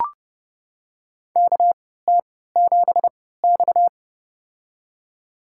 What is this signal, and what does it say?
Morse code, heard as a keyed tone, about 31 wpm, spaced as for 17 wpm, 700 Hz. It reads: KT7X
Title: USA Call Signs - 4 Characters - 3x Character Spacing - 30wpm